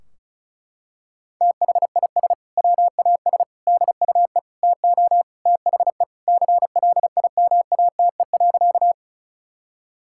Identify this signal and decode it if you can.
Morse code, heard as a keyed tone, about 34 wpm, 700 Hz. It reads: THIS WAS DUE TO THE CLIMATE.
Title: Sentences from Top 1600 Words 35wpm